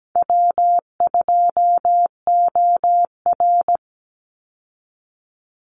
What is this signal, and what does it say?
Morse code, heard as a keyed tone, about 17 wpm, 700 Hz. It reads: W2OR
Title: Easy Call Signs - 17wpm